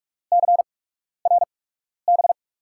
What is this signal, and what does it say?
Morse code, heard as a keyed tone, about 44 wpm, spaced as for 12 wpm, 690 Hz. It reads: CRB
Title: USA Call Signs - 8x Character Spacing - 45wpm